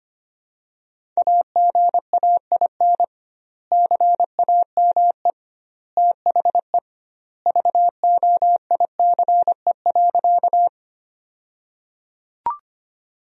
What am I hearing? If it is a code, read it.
Morse code, heard as a keyed tone, about 25 wpm, 700 Hz. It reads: AGAIN CAME THE VOICE.
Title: 2x Word-Spacing - Sentences from Top 500 Words - 25wpm